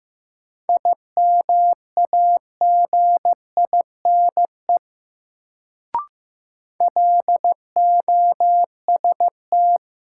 Morse code, heard as a keyed tone, about 15 wpm, 700 Hz. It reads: IMAGINE LOST
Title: Sentences from Top 1400 Words - New Word Drill Alt - 15wpm